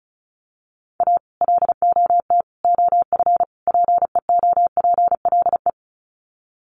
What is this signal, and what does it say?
Morse code, heard as a keyed tone, about 34 wpm, 700 Hz. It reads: A LOT OF PEOPLE
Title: Sets of 4 Words 35wpm